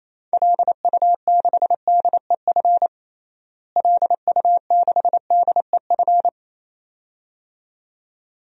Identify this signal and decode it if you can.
Morse code, heard as a keyed tone, about 28 wpm, 700 Hz. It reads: LU6DEF LU6DEF
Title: DXCC-100 Call Signs - 6 Characters - Repeated 2x - 28wpm